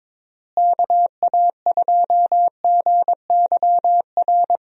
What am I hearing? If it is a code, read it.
Morse code, heard as a keyed tone, about 22 wpm, 700 Hz. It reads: KA2GYR
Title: USA Call Signs - 6 Characters - Repeated 3x - 22wpm